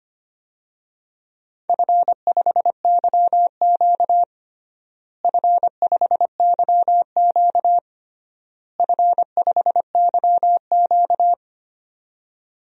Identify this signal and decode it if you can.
Morse code, heard as a keyed tone, about 25 wpm, 700 Hz. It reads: F5YQ F5YQ F5YQ
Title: DXCC-100 Call Signs - 4 Characters - Repeated 3x - 25wpm